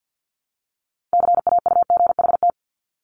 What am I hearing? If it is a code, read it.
Morse code, heard as a keyed tone, about 50 wpm, 700 Hz. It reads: CAUGHT
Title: Multi-letter Phonemes Repeated 3x 50wpm